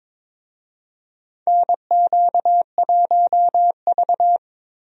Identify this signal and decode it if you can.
Morse code, heard as a keyed tone, about 21 wpm, 700 Hz. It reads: NQ1V
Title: USA Call Signs - 4 Characters - Repeated 3x - 22wpm